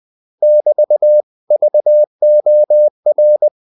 Morse code, heard as a keyed tone, about 20 wpm, 600 Hz. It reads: <BT> VOR